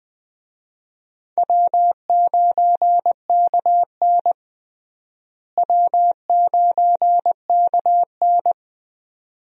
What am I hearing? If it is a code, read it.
Morse code, heard as a keyed tone, about 20 wpm, 700 Hz. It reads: W9KN W9KN
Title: USA Call Signs - 4 Characters - Repeated 2x - 20wpm